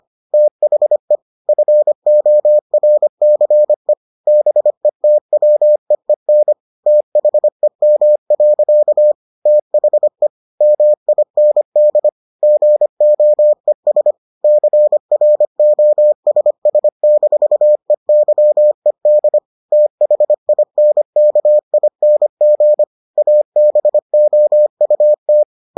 Morse code, heard as a keyed tone, about 25 wpm, 600 Hz. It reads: THE FORCE BETWEEN THEM. THE MIND GOES CROSS-EYED THINKING ABOUT